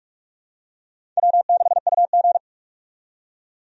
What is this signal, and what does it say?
Morse code, heard as a keyed tone, about 46 wpm, 700 Hz. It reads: W6UG
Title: USA Call Signs - 4 Characters - 45wpm